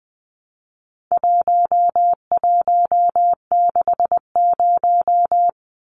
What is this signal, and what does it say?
Morse code, heard as a keyed tone, about 20 wpm, 700 Hz. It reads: 1160
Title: Numbers - 4-Digit 20wpm